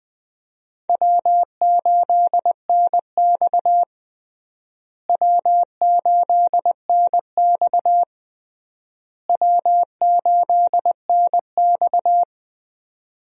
Morse code, heard as a keyed tone, about 21 wpm, 700 Hz. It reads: W8NX W8NX W8NX
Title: USA Call Signs - 4 Characters - Repeated 3x - 20wpm